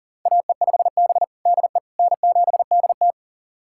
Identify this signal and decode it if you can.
Morse code, heard as a keyed tone, about 40 wpm, 700 Hz. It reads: AE5B DE N7DT
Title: USA Callsign DE Callsign - 4 Characters - 40wpm